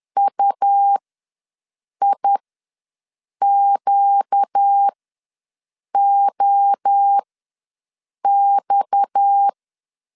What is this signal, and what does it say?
Morse code, heard as a keyed tone, about 10 wpm, spaced as for 6 wpm, 800 Hz. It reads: UIQOX
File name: LEZIONI / 6 PKX0 / 6 PKX ws 10wpm 3706